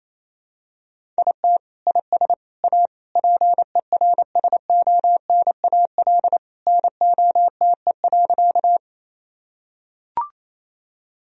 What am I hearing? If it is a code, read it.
Morse code, heard as a keyed tone, about 28 wpm, 700 Hz. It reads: IT IS A PERSONAL NOTE.
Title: Sentences from Top 1000 Words - Review Part 4 - 28wpm